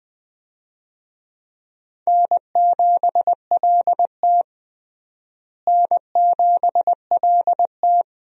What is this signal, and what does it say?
Morse code, heard as a keyed tone, about 20 wpm, 700 Hz. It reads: N7LT N7LT
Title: USA Call Signs - 4 Characters - Repeated 2x - 20wpm